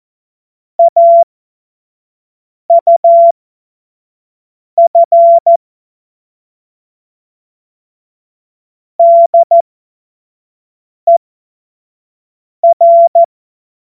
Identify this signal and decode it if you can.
Morse code, heard as a keyed tone, about 14 wpm, spaced as for 5 wpm, 680 Hz. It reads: AUF DER
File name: Morsekurs / Witze / Witz 09. Dezember 2025 25 F70